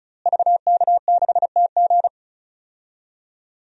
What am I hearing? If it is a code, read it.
Morse code, heard as a keyed tone, about 34 wpm, 700 Hz. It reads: VK6TG
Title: DXCC-100 Call Signs - 5 Characters - Repeated 3x - 35wpm